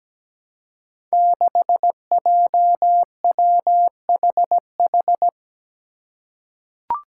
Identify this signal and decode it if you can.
Morse code, heard as a keyed tone, about 17 wpm, 700 Hz. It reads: 6JWHH
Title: Five Challenging Letter-Number Combinations 17wpm